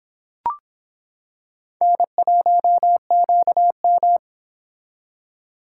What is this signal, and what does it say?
Morse code, heard as a keyed tone, about 26 wpm, 700 Hz. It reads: N1QM